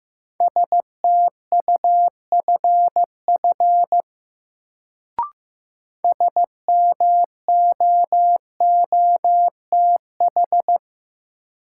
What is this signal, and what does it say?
Morse code, heard as a keyed tone, about 15 wpm, 700 Hz. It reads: STUFF SMOOTH
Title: Sentences from Top 1400 Words - New Word Drill Alt - 15wpm